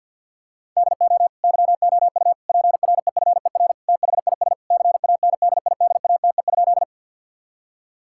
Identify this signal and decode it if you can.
Morse code, heard as a keyed tone, about 50 wpm, 700 Hz. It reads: DO YOU PREFER THIS CANDIDATE?
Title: Sentences from Top 1600 Words 50wpm